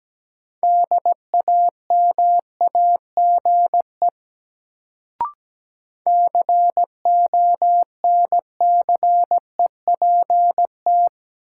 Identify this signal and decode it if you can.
Morse code, heard as a keyed tone, about 17 wpm, 700 Hz. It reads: DAMAGE CONCEPT
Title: Sentences from Top 1100 Words - New Word Drill - 17wpm